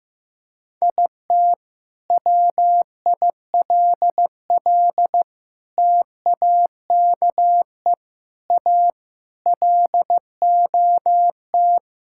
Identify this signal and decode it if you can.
Morse code, heard as a keyed tone, about 16 wpm, 700 Hz. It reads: IT WILL TAKE A LOT
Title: Sentences from Top 1000 Words - Review Part 6 - 15wpm